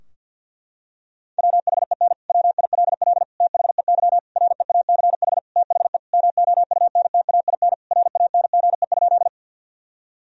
Morse code, heard as a keyed tone, about 50 wpm, 700 Hz. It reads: WHEN WILL THEY REACH THE MOUNTAIN RANGE?
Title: Sentences from Top 1000 Words - Review Part 4 - 50wpm